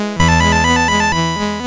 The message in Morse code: none